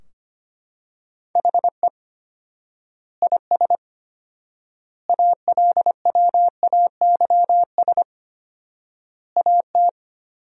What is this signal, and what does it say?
Morse code, heard as a keyed tone, about 25 wpm, 700 Hz. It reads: HE IS ALWAYS AT
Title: 4x Word-Spacing - Sentences from Top 300 Words - 25wpm